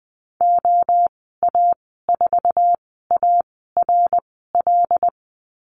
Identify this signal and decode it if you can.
Morse code, heard as a keyed tone, about 20 wpm, spaced as for 14 wpm, 700 Hz. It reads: OA4ARL
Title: 082 Lesson 14g - Number 2 - Call Signs 20wpm